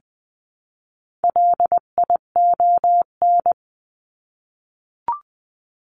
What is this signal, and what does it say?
Morse code, heard as a keyed tone, about 20 wpm, 700 Hz. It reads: LION
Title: 259 Lesson 42d - Words 18-20wpm